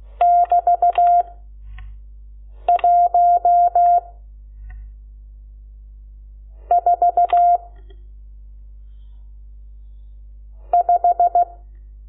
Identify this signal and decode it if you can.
Morse code, heard as a keyed tone, about 16 wpm, 700 Hz. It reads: <BT> 1 4 5
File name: Lecciones / CLASE Nº12